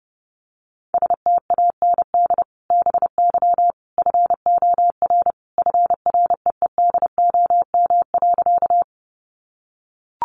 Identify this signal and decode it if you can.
Morse code, heard as a keyed tone, about 30 wpm, 700 Hz. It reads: STAND BY FOR FREEDOM.
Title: Sentences from Top 800 Words 30wpm